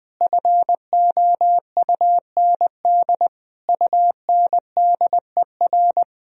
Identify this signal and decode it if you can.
Morse code, heard as a keyed tone, about 21 wpm, 700 Hz. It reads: FOUND UNDER
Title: Sentences from Top 1000 Words - Review Part 1 - 20wpm